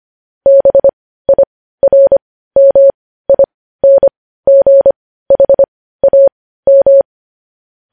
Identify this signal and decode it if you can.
Morse code, heard as a keyed tone, about 25 wpm, spaced as for 15 wpm, 550 Hz. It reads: BIRMINGHAM
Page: news headlines in morse code at 15 wpm